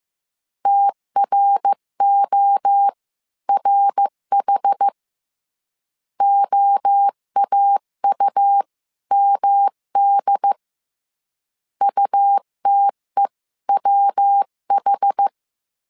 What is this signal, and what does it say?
Morse code, heard as a keyed tone, about 15 wpm, spaced as for 13 wpm, 800 Hz. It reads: TRORH OAUMD UTEWH
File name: CW / DET / 1 3 1500C S C 14wpm 2458